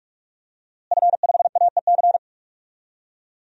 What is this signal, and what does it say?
Morse code, heard as a keyed tone, about 46 wpm, 700 Hz. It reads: F5AEC
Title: DXCC-100 Call Signs - 5 Characters - Repeated 3x - 45wpm